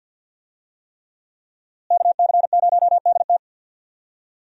Morse code, heard as a keyed tone, about 51 wpm, 700 Hz. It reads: KC0DT
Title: USA Call Signs - 5 Characters - Repeated 3x - 50wpm